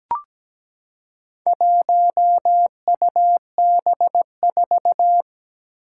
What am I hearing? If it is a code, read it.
Morse code, heard as a keyed tone, about 17 wpm, 700 Hz. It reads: 1UB4
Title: Four Challenging Letter-Number Combinations 17wpm